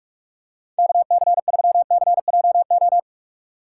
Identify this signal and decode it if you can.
Morse code, heard as a keyed tone, about 43 wpm, 700 Hz. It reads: KC3CJO
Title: USA Call Signs - 2x Character Spacing - 45wpm